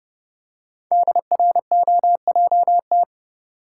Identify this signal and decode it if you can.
Morse code, heard as a keyed tone, about 28 wpm, 690 Hz. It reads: DROJT
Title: Code Groups - 5 Letters - 30wpm